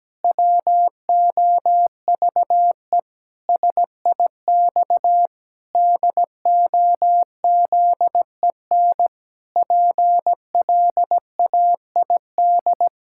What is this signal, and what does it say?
Morse code, heard as a keyed tone, about 17 wpm, 700 Hz. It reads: WOVE SIX DOZEN PLAID
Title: Pangram Challenge v2 - 17wpm